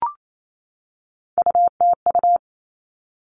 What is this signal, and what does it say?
Morse code, heard as a keyed tone, about 28 wpm, 700 Hz. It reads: UTU
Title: All Cut Numbers - 3-Digit 28wpm